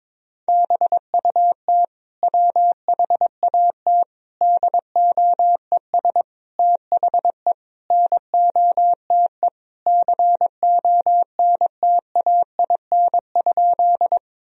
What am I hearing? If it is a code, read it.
Morse code, heard as a keyed tone, about 22 wpm, 700 Hz. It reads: BUT WHAT DOES THE NOTE CONTAIN?
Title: Sentences from Top 1000 Words - Review Part 2 - 22wpm